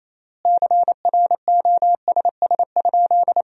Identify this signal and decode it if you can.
Morse code, heard as a keyed tone, about 28 wpm, 700 Hz. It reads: CROSS?
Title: 3x Word-Spacing - Sentences from Top 300 Words - 28wpm